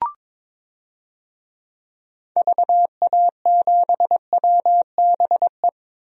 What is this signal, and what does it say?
Morse code, heard as a keyed tone, about 22 wpm, 700 Hz. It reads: VA7WBE
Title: DXCC-100 Call Signs - 6 Characters - 22wpm